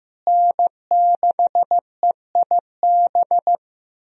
Morse code, heard as a keyed tone, about 15 wpm, 700 Hz. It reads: N6EIB
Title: USA Call Signs - 5 Characters - Repeated 3x - 15wpm